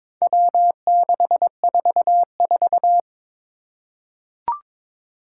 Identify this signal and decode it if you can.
Morse code, heard as a keyed tone, about 22 wpm, 700 Hz. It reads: W644
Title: Four Challenging Letter-Number Combinations 22wpm